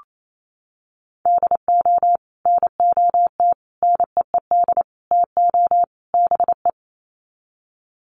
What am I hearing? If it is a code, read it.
Morse code, heard as a keyed tone, about 28 wpm, 700 Hz. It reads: DO NOT NEED TO BE
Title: Sets of 5 Words Encore 28wpm